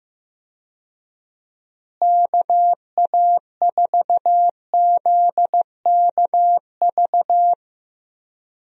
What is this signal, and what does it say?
Morse code, heard as a keyed tone, about 15 wpm, 700 Hz. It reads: KA4ZKV
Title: USA Call Signs - 6 Characters - Repeated 3x - 15wpm